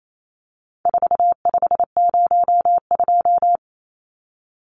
Morse code, heard as a keyed tone, about 28 wpm, 700 Hz. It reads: 4502
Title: Numbers - 4-Digit 28wpm